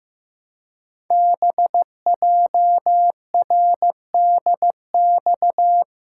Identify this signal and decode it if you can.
Morse code, heard as a keyed tone, about 15 wpm, 700 Hz. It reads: BJRDX
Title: Code Groups - 5 Letters - 15wpm